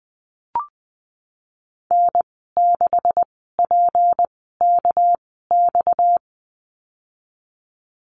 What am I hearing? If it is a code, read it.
Morse code, heard as a keyed tone, about 20 wpm, spaced as for 15 wpm, 700 Hz. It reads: N6PKX
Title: 238 Lesson 38g - Letter q - Call Signs 20wpm